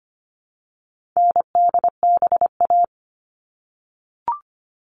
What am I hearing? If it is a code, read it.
Morse code, heard as a keyed tone, about 25 wpm, 700 Hz. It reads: NDBA
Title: All Cut Numbers - 4-Digit 25wpm